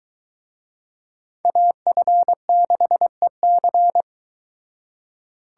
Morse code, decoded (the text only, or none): AF6EC